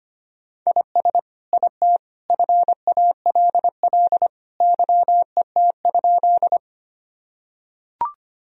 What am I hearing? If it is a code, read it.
Morse code, heard as a keyed tone, about 25 wpm, 700 Hz. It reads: IS IT FALL YET?
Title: Sentences from Top 1000 Words - Review Part 2 - 25wpm